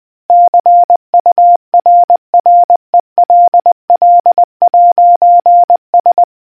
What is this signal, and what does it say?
Morse code, heard as a keyed tone, about 20 wpm, 700 Hz. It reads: CURRELL'S